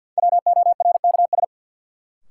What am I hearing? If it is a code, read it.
Morse code, heard as a keyed tone, about 53 wpm, 700 Hz. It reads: WORKS
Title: Sentences from Top 1100 Words - New Word Drill Alt - 50wpm